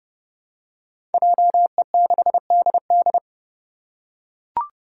Morse code, heard as a keyed tone, about 30 wpm, 700 Hz. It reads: JE6DD
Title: Five Challenging Letter-Number Combinations 30wpm